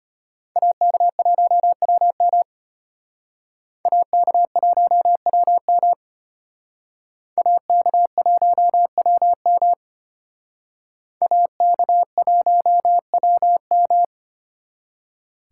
Morse code, heard as a keyed tone, about 32 wpm, 700 Hz. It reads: AK1WM AK1WM AK1WM AK1WM